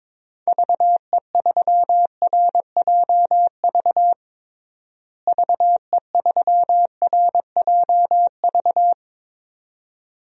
Morse code, heard as a keyed tone, about 22 wpm, 700 Hz. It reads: VE3RJV VE3RJV
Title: DXCC-100 Call Signs - 6 Characters - Repeated 2x - 22wpm